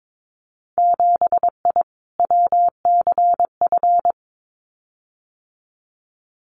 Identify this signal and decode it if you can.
Morse code, heard as a keyed tone, about 22 wpm, 700 Hz. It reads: 7I WCF
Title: Winter Field Day Exchanges Encore 22wpm